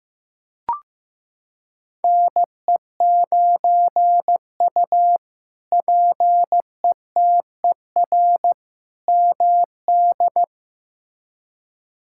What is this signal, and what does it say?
Morse code, heard as a keyed tone, about 15 wpm, 700 Hz. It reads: NE9U PETER MD
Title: K1USN SST QSO Practice - 15wpm